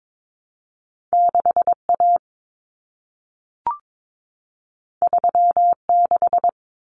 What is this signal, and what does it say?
Morse code, heard as a keyed tone, about 22 wpm, 700 Hz. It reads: 6A 36
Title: Common Cut Numbers - 2-Digit 22wpm